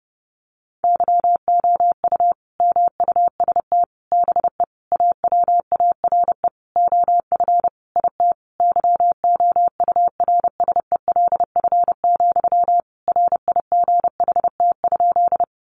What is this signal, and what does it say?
Morse code, heard as a keyed tone, about 28 wpm, 700 Hz. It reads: YOU MUST BE AWARE OF IT YOURSELF, RIGHT?
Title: Sentences from Top 1200 Words 30wpm